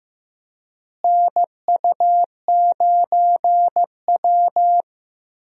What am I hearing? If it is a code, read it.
Morse code, heard as a keyed tone, about 15 wpm, 700 Hz. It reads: NU9W
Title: USA Call Signs - 4 Characters - Repeated 3x - 15wpm